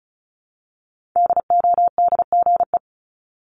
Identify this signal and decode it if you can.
Morse code, heard as a keyed tone, about 36 wpm, 700 Hz. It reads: DODGE
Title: Multi-letter Phonemes 35wpm